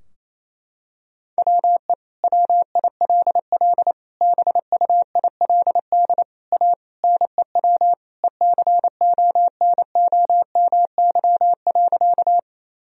Morse code, heard as a keyed tone, about 28 wpm, 700 Hz. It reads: WE WILL BUILD A NEW ECONOMY.
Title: Sentences from Top 1000 Words - Review Part 3 - 28wpm